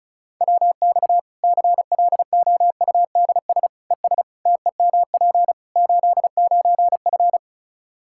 Morse code, heard as a keyed tone, about 34 wpm, 700 Hz. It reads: WX CLOUDS ES TEMP 89F
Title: Practice QSO - Exchange 2 - Info - 35wpm